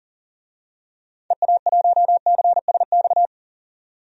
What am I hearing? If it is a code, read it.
Morse code, heard as a keyed tone, about 40 wpm, 700 Hz. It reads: EA1CSX
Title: DXCC-100 Call Signs - 6 Characters - 40wpm